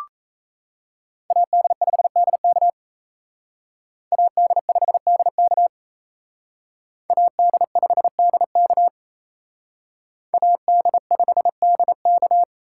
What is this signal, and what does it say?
Morse code, heard as a keyed tone, about 33 wpm, 700 Hz. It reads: AD5DK AD5DK AD5DK AD5DK
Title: Speed Racing - USA 2x2 Call Signs 28wpm